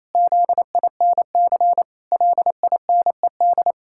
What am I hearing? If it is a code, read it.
Morse code, heard as a keyed tone, about 28 wpm, 700 Hz. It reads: ZINC LINED
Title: Pangram Challenge v2 - 28wpm